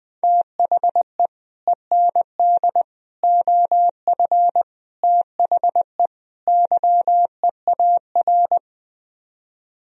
Morse code, heard as a keyed tone, about 20 wpm, 700 Hz. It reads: THE END OF THE YEAR
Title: Speed-Racing - Sets of 5 Words 20wpm